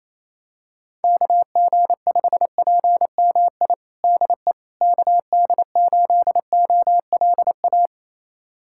Morse code, heard as a keyed tone, about 28 wpm, 700 Hz. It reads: KG5PMI DE KD8OLA
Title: USA Callsign DE Callsign - 6 Characters - 28wpm